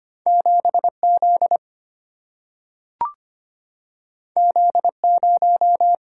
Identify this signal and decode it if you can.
Morse code, heard as a keyed tone, about 25 wpm, 700 Hz. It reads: 7Z Z0
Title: Single-Letter Single-Number 25wpm